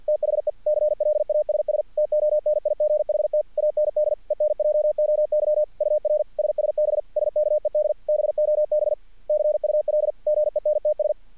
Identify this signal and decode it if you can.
Morse code, heard as a keyed tone, about 46 wpm, 600 Hz. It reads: THE CLASS TONIGHT AND ENJOY UR SSB SKED BOB CUL GENTS
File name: archived / CRSnet